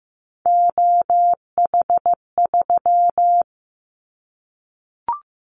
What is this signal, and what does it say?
Morse code, heard as a keyed tone, about 15 wpm, 700 Hz. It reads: OH3
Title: Callsign Prefixes for Top 25 DXCC - 15wpm